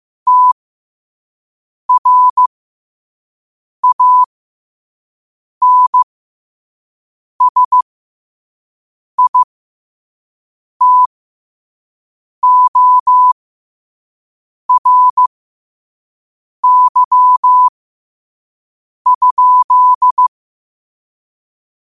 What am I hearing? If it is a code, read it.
Morse code, heard as a keyed tone, about 15 wpm, spaced as for 5 wpm, 1000 Hz. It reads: TRANSITORY?
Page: New quotes every day in morse code at 5 Words per minute.